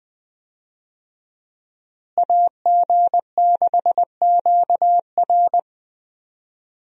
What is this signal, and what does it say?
Morse code, heard as a keyed tone, about 20 wpm, 700 Hz. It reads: AG6QR
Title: USA Call Signs - 5 Characters - Repeated 3x - 20wpm